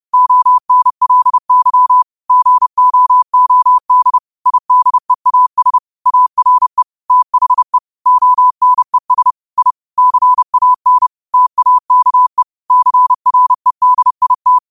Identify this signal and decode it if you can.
Morse code, heard as a keyed tone, about 30 wpm, 1000 Hz. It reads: ONLY GOOD IDEAS ARE THE ONES I CAN TAKE CREDIT